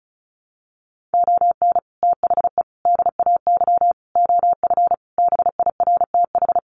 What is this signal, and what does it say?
Morse code, heard as a keyed tone, about 34 wpm, 700 Hz. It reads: ON THE DAY OF BIRTH
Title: Sentences from Top 1100 Words 35wpm